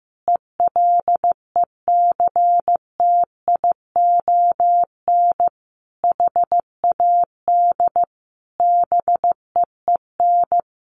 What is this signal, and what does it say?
Morse code, heard as a keyed tone, about 15 wpm, 700 Hz. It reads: ELECTION HAD BEEN
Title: Sentences from Top 1500 Words 15wpm